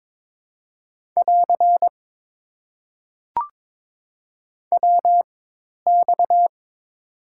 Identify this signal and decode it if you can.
Morse code, heard as a keyed tone, about 22 wpm, spaced as for 10 wpm, 700 Hz. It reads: <AR> WX
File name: Top 90 QSO Elements - 4x Character Spacing - 22wpm